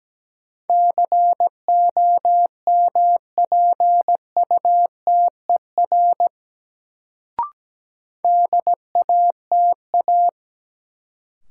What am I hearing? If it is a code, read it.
Morse code, heard as a keyed tone, about 17 wpm, 700 Hz. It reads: COMPUTER DATA
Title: Sentences from Top 1000 Words - Review Part 3 - New Word Drill - 17wpm